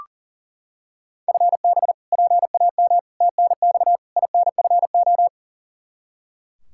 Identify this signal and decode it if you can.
Morse code, heard as a keyed tone, about 40 wpm, 700 Hz. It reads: FB PAM TNX INFO
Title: K1USN SST QSO Practice 40wpm